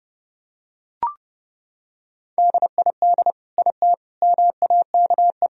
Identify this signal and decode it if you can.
Morse code, heard as a keyed tone, about 32 wpm, 700 Hz. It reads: DID IT MAKE
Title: Sentences from Top 1000 Words - Review Part 4 - 30wpm